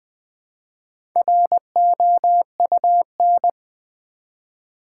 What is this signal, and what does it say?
Morse code, heard as a keyed tone, about 20 wpm, 700 Hz. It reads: ROUN
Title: Build A Word - 1 Letter At A Time - 5-Letter Words - 20wpm